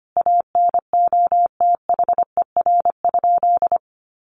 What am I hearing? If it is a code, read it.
Morse code, heard as a keyed tone, about 25 wpm, 700 Hz. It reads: ANOTHER?
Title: Sentences from Top 1400 Words 25wpm